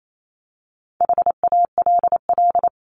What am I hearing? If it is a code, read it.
Morse code, heard as a keyed tone, about 28 wpm, 700 Hz. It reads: HALL
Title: Top 1751-2000 Words 28wpm